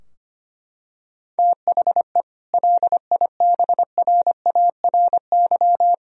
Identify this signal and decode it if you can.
Morse code, heard as a keyed tone, about 25 wpm, 700 Hz. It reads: THE LIBRARY
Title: Sentences from Top 1000 Words - Review Part 4 - 25wpm